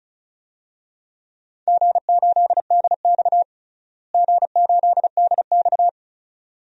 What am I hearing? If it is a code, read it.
Morse code, heard as a keyed tone, about 34 wpm, 700 Hz. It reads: G8DX G8DX
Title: DXCC-100 Call Signs - 4 Characters - Repeated 2x - 35wpm